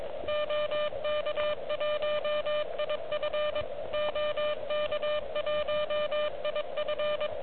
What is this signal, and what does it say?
Morse code, heard as a keyed tone, about 23 wpm, 590 Hz, with noise. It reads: OK1IF OK1IF
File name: OK QRP 2022 / Zvuk / REC36